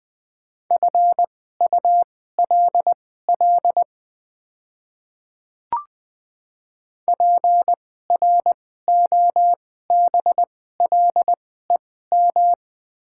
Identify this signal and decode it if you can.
Morse code, heard as a keyed tone, about 20 wpm, spaced as for 15 wpm, 700 Hz. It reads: FULL PROBLEM
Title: 2x Character Spacing - Top 500 Words - 20wpm